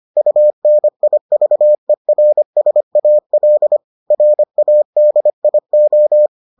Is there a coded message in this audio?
Yes: Morse code from 0.2 to 6.3 s, heard as a keyed tone, about 25 wpm, 600 Hz. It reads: UNIVERSAL RADIO